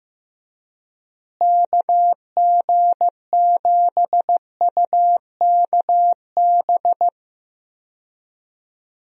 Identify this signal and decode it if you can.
Morse code, heard as a keyed tone, about 15 wpm, 700 Hz. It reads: KG7UKB
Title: USA Call Signs - 6 Characters - Repeated 3x - 15wpm